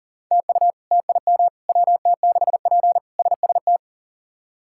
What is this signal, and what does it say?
Morse code, heard as a keyed tone, about 40 wpm, 700 Hz. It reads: TU TIM WT6P SST
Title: K1USN SST QSO Practice - 40wpm